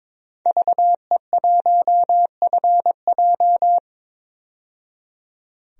Morse code, heard as a keyed tone, about 21 wpm, 700 Hz. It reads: VE1FJ
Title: DXCC-100 Call Signs - 5 Characters - 22wpm